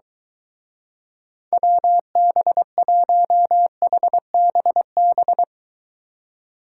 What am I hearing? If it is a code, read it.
Morse code, heard as a keyed tone, about 23 wpm, 700 Hz. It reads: WB1HBB